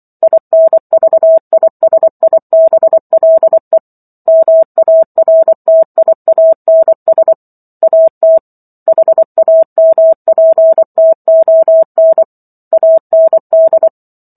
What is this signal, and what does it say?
Morse code, heard as a keyed tone, about 24 wpm, 650 Hz. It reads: INVISIBLE MARTIANS AT HAMPTON AND